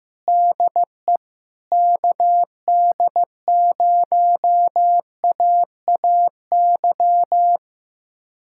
Morse code, heard as a keyed tone, about 15 wpm, 700 Hz. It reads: DE KD0AAY
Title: USA Callsign DE Callsign 15wpm